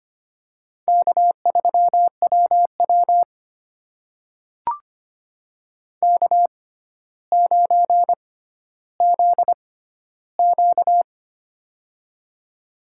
Morse code, heard as a keyed tone, about 25 wpm, spaced as for 13 wpm, 700 Hz. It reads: K3WW K9ZQ
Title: USA Call Signs - 4 Characters - 6x Character Spacing - 25wpm